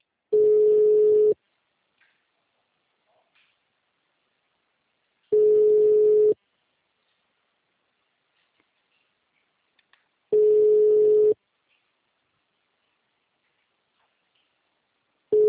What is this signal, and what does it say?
Ringback tone (Europe, 425 Hz).